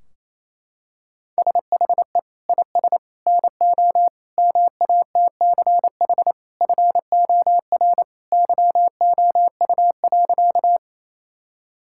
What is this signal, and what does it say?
Morse code, heard as a keyed tone, about 28 wpm, 700 Hz. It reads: SHE IS NO MATCH FOR YOU.
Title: Sentences from Top 1600 Words 28wpm